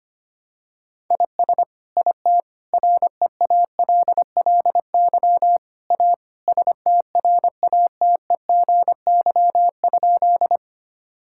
Morse code, heard as a keyed tone, about 25 wpm, 700 Hz. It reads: IS IT REALLY A STRATEGY?
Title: Sentences from Top 1000 Words - Review Part 5 - 25wpm